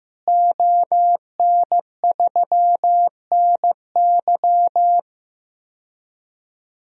Morse code, heard as a keyed tone, about 15 wpm, 700 Hz. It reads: ON3NY
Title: DXCC-100 Call Signs - 5 Characters - Repeated 3x - 15wpm